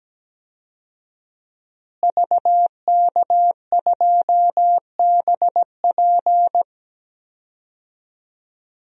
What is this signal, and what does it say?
Morse code, heard as a keyed tone, about 17 wpm, 700 Hz. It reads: VK2BP
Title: DXCC-100 Call Signs - 5 Characters - Repeated 2x - 17wpm